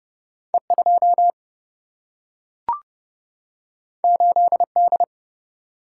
Morse code, heard as a keyed tone, about 30 wpm, 700 Hz. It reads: E 2 8 D
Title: Single-Letter Single-Number 30wpm